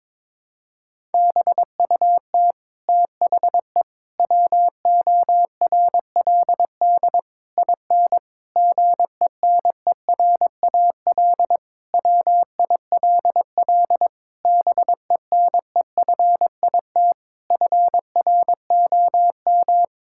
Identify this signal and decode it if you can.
Morse code, heard as a keyed tone, about 22 wpm, 700 Hz. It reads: BUT THE WORLD IN GENERAL WILL BENEFIT FROM
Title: Sentences from Top 1000 Words - Review Part 6 - 22wpm